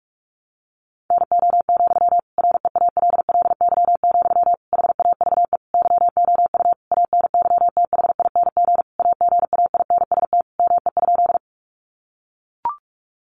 Morse code, heard as a keyed tone, about 44 wpm, 700 Hz. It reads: NO, REALLY, HAVE YOU ANYTHING AGAINST ME?
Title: Sentences from Top 1000 Words 45wpm